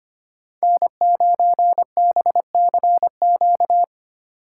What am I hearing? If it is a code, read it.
Morse code, heard as a keyed tone, about 25 wpm, 700 Hz. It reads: N9BCQ
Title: USA Call Signs - 5 Characters - 3x Character Spacing - 25wpm